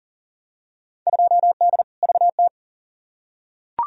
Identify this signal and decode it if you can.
Morse code, heard as a keyed tone, about 40 wpm, 700 Hz. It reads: 2D VT
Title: Field Day Exchanges Encore - 40wpm